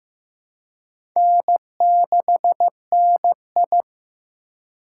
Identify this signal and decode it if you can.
Morse code, heard as a keyed tone, about 15 wpm, 700 Hz. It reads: N6NI
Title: USA Call Signs - 4x Character Spacing - 15wpm